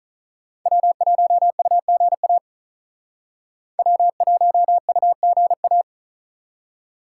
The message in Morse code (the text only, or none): W1UGA W1UGA